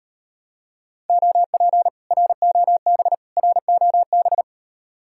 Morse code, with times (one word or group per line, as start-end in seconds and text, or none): 1.1-1.9 OP
2.1-3.1 ROB
3.4-4.4 ROB